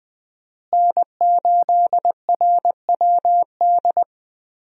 Morse code, heard as a keyed tone, about 21 wpm, 700 Hz. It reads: N8RWD
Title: USA Call Signs - 5 Characters - Repeated 2x - 20wpm